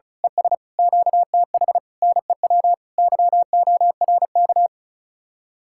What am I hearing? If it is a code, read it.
Morse code, heard as a keyed tone, about 33 wpm, 700 Hz. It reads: ES QTH NEW YORK
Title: Speed-Racing Common QSO Phrases - 35wpm